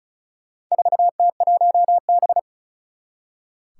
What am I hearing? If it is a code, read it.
Morse code, heard as a keyed tone, about 36 wpm, 700 Hz. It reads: 4T1B
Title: Four Challenging Letter-Number Combinations 35wpm